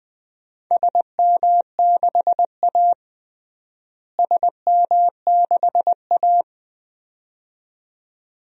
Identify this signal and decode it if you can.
Morse code, heard as a keyed tone, about 20 wpm, 700 Hz. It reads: SM6A SM6A
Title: DXCC-100 Call Signs - 4 Characters - Repeated 2x - 20wpm